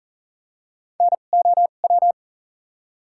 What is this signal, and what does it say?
Morse code, heard as a keyed tone, about 37 wpm, spaced as for 28 wpm, 700 Hz. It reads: NOW
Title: 2x Character Spacing - Top 300 Words - 40wpm